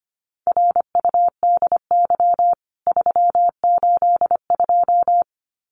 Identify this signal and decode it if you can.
Morse code, heard as a keyed tone, about 25 wpm, 700 Hz. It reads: RUDY 382
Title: CWT Exchange 25wpm